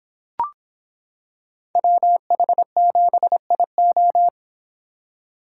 Morse code, heard as a keyed tone, about 26 wpm, 700 Hz. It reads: WH7IO